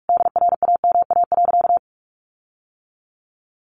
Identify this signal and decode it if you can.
Morse code, heard as a keyed tone, about 44 wpm, 700 Hz. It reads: DRAMA.
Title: Sentences from Top 1500 Words 45wpm